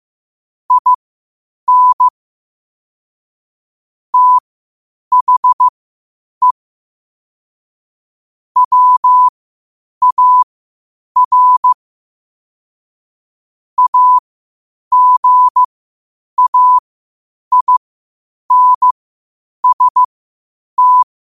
Morse code, heard as a keyed tone, about 15 wpm, spaced as for 8 wpm, 1000 Hz. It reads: IN THE WAR AGAINST
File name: qotdfiles / 8wpm